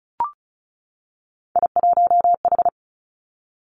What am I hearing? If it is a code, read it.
Morse code, heard as a keyed tone, about 36 wpm, 700 Hz. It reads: I1H
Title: Three Challenging Letter-Number Combinations 35wpm